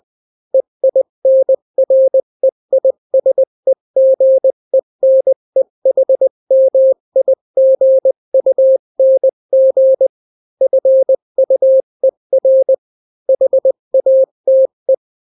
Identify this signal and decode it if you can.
Morse code, heard as a keyed tone, about 20 wpm, 530 Hz. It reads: EINREISEGENEHMIGUNG FUER HATE
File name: daily / 18wpm